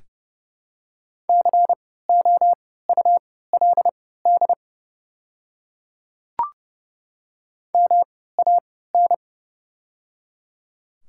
Morse code, heard as a keyed tone, about 30 wpm, spaced as for 17 wpm, 690 Hz. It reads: COULD MAN
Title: 3x Character Spacing - Top 200 Words - 30wpm